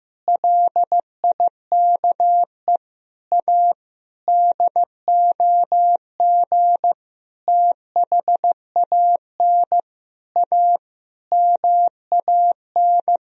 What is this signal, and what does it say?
Morse code, heard as a keyed tone, about 15 wpm, 700 Hz. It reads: LIKE A DOG THAN A MAN
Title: Sentences from Top 1000 Words - Review Part 2 - 15wpm